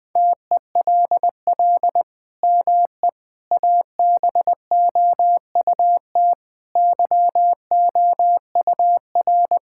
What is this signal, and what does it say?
Morse code, heard as a keyed tone, about 21 wpm, 700 Hz. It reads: TELL ME ABOUT YOUR
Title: Sentences from Top 1000 Words - Review Part 5 - 20wpm